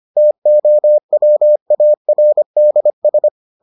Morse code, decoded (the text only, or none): TOWARDS